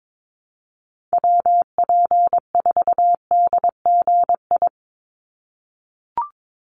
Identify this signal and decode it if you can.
Morse code, heard as a keyed tone, about 22 wpm, 700 Hz. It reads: WP4DGI
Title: USA Call Signs 22wpm